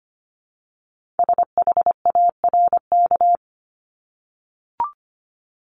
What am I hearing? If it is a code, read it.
Morse code, heard as a keyed tone, about 25 wpm, 700 Hz. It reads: SHARK
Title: Multi-letter Phonemes Repeated 3x 25wpm